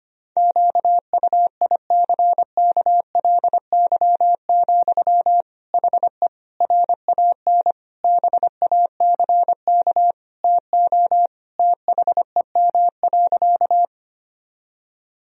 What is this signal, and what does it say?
Morse code, heard as a keyed tone, about 25 wpm, 700 Hz. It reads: QUICKLY, HE RAN BACK TO THEM.
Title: Sentences from Top 1000 Words - Review Part 3 - 25wpm